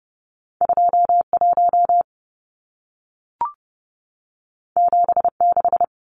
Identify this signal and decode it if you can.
Morse code, heard as a keyed tone, about 30 wpm, 700 Hz. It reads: 21 76
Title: Numbers - 2-Digit 30wpm